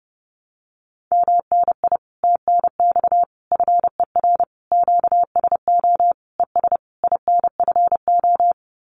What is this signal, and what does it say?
Morse code, heard as a keyed tone, about 30 wpm, 700 Hz. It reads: MNI TNX FER QSO ES INFO
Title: Common QSO Phrases Encore 30wpm